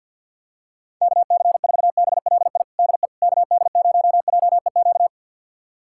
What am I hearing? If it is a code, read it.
Morse code, heard as a keyed tone, about 52 wpm, 700 Hz. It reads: KC4BLI DE KD0JEQ
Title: USA Callsign DE Callsign 50wpm